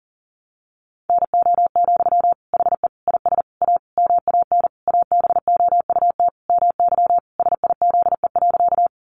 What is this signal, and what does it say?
Morse code, heard as a keyed tone, about 40 wpm, 700 Hz. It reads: NO, HE IS A MAN ABOUT MY SIZE.
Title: Sentences from Top 400 Words 40wpm